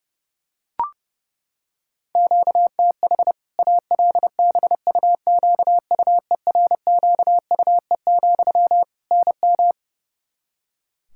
Morse code, heard as a keyed tone, about 32 wpm, 700 Hz. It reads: QTH ALBUQUERQUE, NM
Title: Practice QSO - Exchange 1 - Beginning - 30wpm